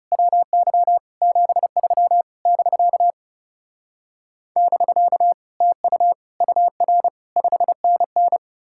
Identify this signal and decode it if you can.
Morse code, heard as a keyed tone, about 40 wpm, 700 Hz. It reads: WY 73 <BK> <BK> TU UR 5NN